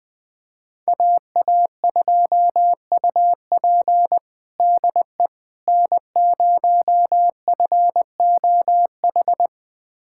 Morse code, heard as a keyed tone, about 20 wpm, 700 Hz. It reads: AA2UP DE N0FOH
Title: USA Callsign DE Callsign - 5 Characters - 20wpm